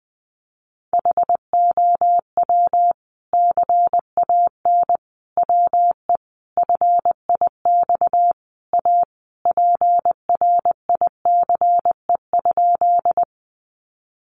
Morse code, heard as a keyed tone, about 20 wpm, 700 Hz. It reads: HOW CAN WE FIX A PRICE?
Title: Sentences from Top 1400 Words 20wpm